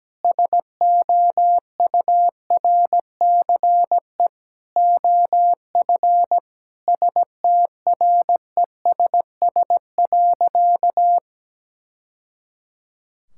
Morse code, heard as a keyed tone, about 17 wpm, 700 Hz. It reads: SOURCE OF STRESS.
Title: Sentences from Top 1000 Words - Review Part 6 - 17wpm